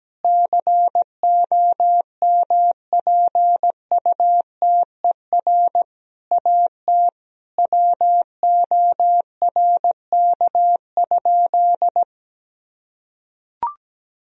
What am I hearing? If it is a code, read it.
Morse code, heard as a keyed tone, about 17 wpm, 700 Hz. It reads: COMPUTER AT WORK?
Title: Sentences from Top 1000 Words - Review Part 3 - 17wpm